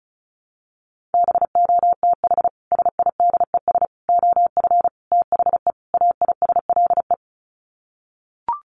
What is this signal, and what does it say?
Morse code, heard as a keyed tone, about 34 wpm, 700 Hz. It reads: BOTH SIDES OF THE AISLE
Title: Sets of 5 Words Encore2 35wpm